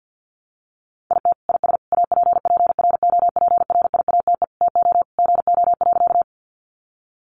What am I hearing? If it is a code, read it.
Morse code, heard as a keyed tone, about 50 wpm, 700 Hz. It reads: IT IS APPROPRIATE TO GO.
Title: Sentences from Top 900 Words 50wpm